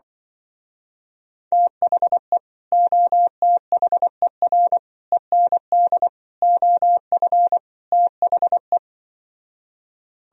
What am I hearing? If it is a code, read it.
Morse code, heard as a keyed tone, about 25 wpm, 700 Hz. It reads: THE OTHER END OF THE